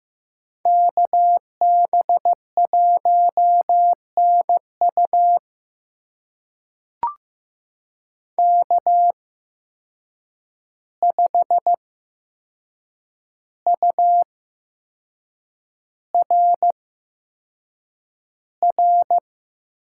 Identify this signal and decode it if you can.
Morse code, heard as a keyed tone, about 15 wpm, spaced as for 6 wpm, 700 Hz. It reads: KB1NU K5URR
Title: USA Call Signs - 5 Characters - 8x Character Spacing - 15wpm